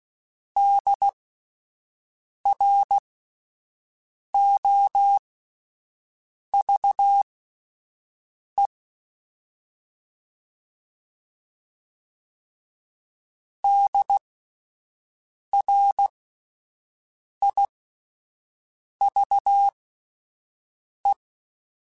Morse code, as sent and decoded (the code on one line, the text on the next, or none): -.. .-. --- ...- . / -.. .-. .. ...- .
DROVE DRIVE